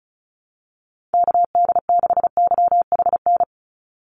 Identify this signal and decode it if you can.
Morse code, heard as a keyed tone, about 36 wpm, 700 Hz. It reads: KD6YHN
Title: USA Call Signs 35wpm